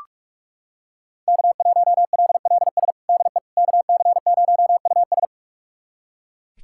Morse code, heard as a keyed tone, about 41 wpm, 700 Hz. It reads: K1LLS DE KC0US
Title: USA Callsign DE Callsign - 5 Characters - 45wpm